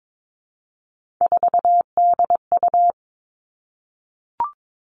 Morse code, heard as a keyed tone, about 22 wpm, 700 Hz. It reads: 4DU
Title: Three Challenging Letter-Number Combinations 22wpm